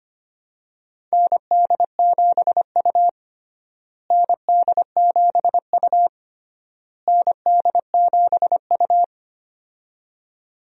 Morse code, heard as a keyed tone, about 25 wpm, 700 Hz. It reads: ND7U ND7U ND7U
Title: USA Call Signs - 4 Characters - Repeated 3x - 25wpm